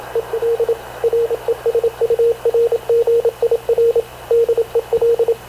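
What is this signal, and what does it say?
Morse code, heard as a keyed tone, about 26 wpm, 460 Hz, with radio noise. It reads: EL RESURGIR DEL
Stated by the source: Havana Radio / CLA sending Spanish press (PX) in Morse, recorded by Radio Officer Paul Dumesnil onboard CNAV QUEST/CZDO.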